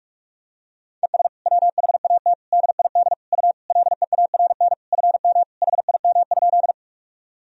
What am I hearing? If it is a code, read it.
Morse code, heard as a keyed tone, about 46 wpm, 700 Hz. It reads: ES WHAT DID U LEARN FM HIM?
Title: Sentences from Top 300 Words and Abbreviations 45wpm